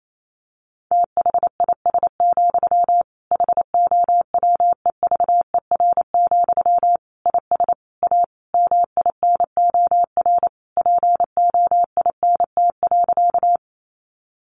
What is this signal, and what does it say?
Morse code, heard as a keyed tone, about 28 wpm, 700 Hz. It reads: THIS, HOWEVER, IS A MINOR POINT.
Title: Sentences from Top 1500 Words 28wpm